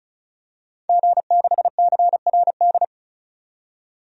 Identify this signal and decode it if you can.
Morse code, heard as a keyed tone, about 36 wpm, 700 Hz. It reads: G6CRD
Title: DXCC-100 Call Signs - 5 Characters - 35wpm